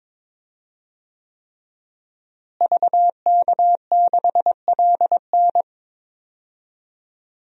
Morse code, heard as a keyed tone, about 22 wpm, 700 Hz. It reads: VK6LN
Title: DXCC-100 Call Signs - 5 Characters - Repeated 3x - 22wpm